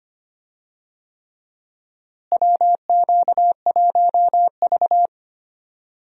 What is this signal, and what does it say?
Morse code, heard as a keyed tone, about 25 wpm, 700 Hz. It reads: WQ1V
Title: USA Call Signs - 4 Characters - Repeated 3x - 25wpm